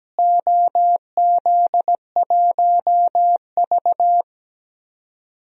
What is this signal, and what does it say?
Morse code, heard as a keyed tone, about 17 wpm, 700 Hz. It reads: OZ1V
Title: DXCC-100 Call Signs - 4 Characters - Repeated 2x - 17wpm